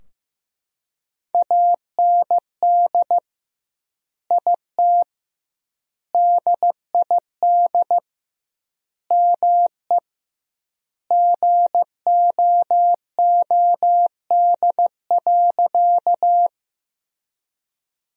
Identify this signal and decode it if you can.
Morse code, heard as a keyed tone, about 15 wpm, 700 Hz. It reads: AND IT DID ME GOOD.
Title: 2x Word-Spacing - Sentences from Top 200 Words - 15wpm